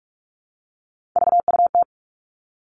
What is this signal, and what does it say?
Morse code, heard as a keyed tone, about 46 wpm, 700 Hz. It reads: VUT
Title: Three Challenging Letter-Number Combinations 45wpm